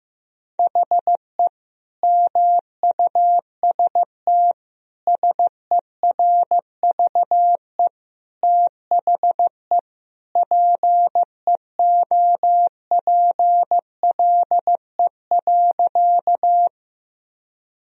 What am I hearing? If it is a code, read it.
Morse code, heard as a keyed tone, about 15 wpm, 700 Hz. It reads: HE MUST SERVE THE PEOPLE.
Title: Sentences from Top 1000 Words - Review Part 2 - 15wpm